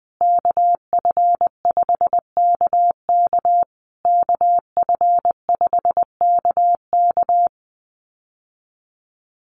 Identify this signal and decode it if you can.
Morse code, heard as a keyed tone, about 20 wpm, 700 Hz. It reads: KF5KK KF5KK
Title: USA Call Signs Repeated 3x 20wpm